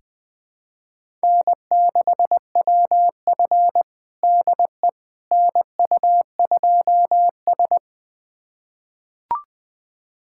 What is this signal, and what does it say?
Morse code, heard as a keyed tone, about 20 wpm, 700 Hz. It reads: N6WF DE NU2S
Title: USA Callsign DE Callsign - 4 Characters - 20wpm